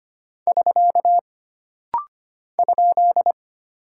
Morse code, heard as a keyed tone, about 25 wpm, 700 Hz. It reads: <SK> ?
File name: Single Punctuation-Prosign - Rapid-Fire - 25wpm